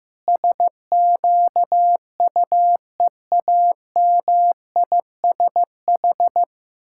Morse code, heard as a keyed tone, about 15 wpm, 700 Hz. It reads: SQUEAMISH
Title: Pangram Challenge v2 - 15wpm